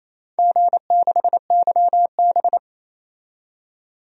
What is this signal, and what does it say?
Morse code, heard as a keyed tone, about 28 wpm, 700 Hz. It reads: G6YB
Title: DXCC-100 Call Signs - 4 Characters - 28wpm